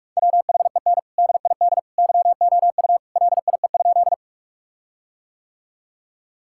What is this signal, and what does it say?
Morse code, heard as a keyed tone, about 43 wpm, 700 Hz. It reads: WHEN DID YOU LIE?
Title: Sentences from Top 1600 Words 45wpm